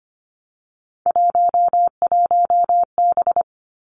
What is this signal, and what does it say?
Morse code, heard as a keyed tone, about 25 wpm, 700 Hz. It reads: 11B
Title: Three Challenging Letter-Number Combinations 25wpm